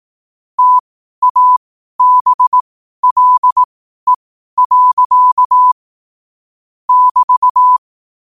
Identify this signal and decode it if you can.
Morse code, heard as a keyed tone, about 18 wpm, spaced as for 12 wpm, 1000 Hz. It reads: TABLE. <BT>
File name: qotdfiles / 12wpm